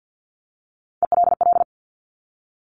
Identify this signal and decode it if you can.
Morse code, heard as a keyed tone, about 50 wpm, 700 Hz. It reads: ELL
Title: Top Three Letter Combinations in Words 50wpm